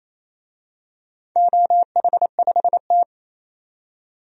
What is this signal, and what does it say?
Morse code, heard as a keyed tone, about 27 wpm, 700 Hz. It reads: OH5T
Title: DXCC-100 Call Signs - 4 Characters - 28wpm